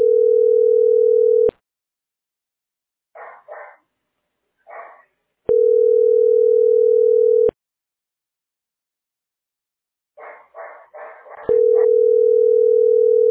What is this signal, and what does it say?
Ringback tone (North America), 440 + 480 Hz.